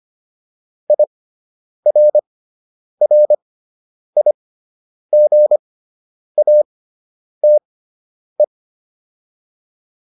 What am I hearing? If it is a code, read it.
Morse code, heard as a keyed tone, about 25 wpm, spaced as for 9 wpm, 600 Hz. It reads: IRRIGATE